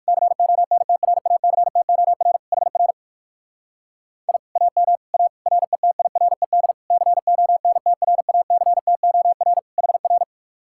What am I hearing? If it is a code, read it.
Morse code, heard as a keyed tone, about 62 wpm, 700 Hz. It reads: CONTRACTOR HR I AM A RETIRED CONTRACTOR HR